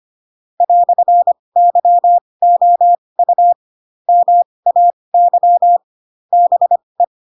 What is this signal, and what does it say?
Morse code, heard as a keyed tone, about 25 wpm, spaced as for 20 wpm, 700 Hz. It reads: "YOU MAY BE
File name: Pinocchio / Pinocchio ch7 15 25 ik1hge